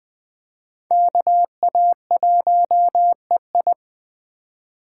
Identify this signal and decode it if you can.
Morse code, heard as a keyed tone, about 21 wpm, 700 Hz. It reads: KA1EI
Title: Speed Racing - USA 2x2 Call Signs 17wpm